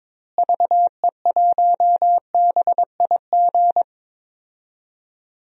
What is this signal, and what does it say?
Morse code, heard as a keyed tone, about 22 wpm, 700 Hz. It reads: VE1BIG
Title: DXCC-100 Call Signs - 6 Characters - Repeated 3x - 22wpm